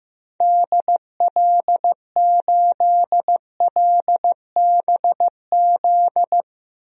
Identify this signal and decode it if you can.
Morse code, heard as a keyed tone, about 15 wpm, 700 Hz. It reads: DL8LBZ
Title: DXCC-100 Call Signs - 6 Characters - 15wpm